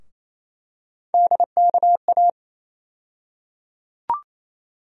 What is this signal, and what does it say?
Morse code, heard as a keyed tone, about 28 wpm, 700 Hz. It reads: DKA
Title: USA Call Sign Postfixes - 3 Letters 28wpm